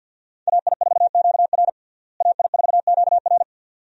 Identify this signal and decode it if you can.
Morse code, heard as a keyed tone, about 49 wpm, 700 Hz. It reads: AI4QR AI4QR
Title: USA Call Signs - 5 Characters - Repeated 2x - 50wpm